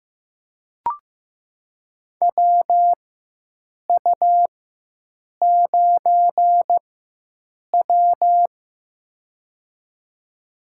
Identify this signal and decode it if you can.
Morse code, heard as a keyed tone, about 15 wpm, spaced as for 7 wpm, 700 Hz. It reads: WU9W
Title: USA Call Signs - 4 Characters - 4x Character Spacing - 15wpm